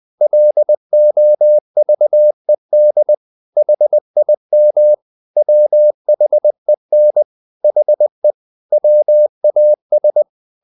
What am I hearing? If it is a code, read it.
Morse code, heard as a keyed tone, about 20 wpm, 600 Hz. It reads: LOVED HIM WHEN HE WAS
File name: the old man and the sea part 2 20wpm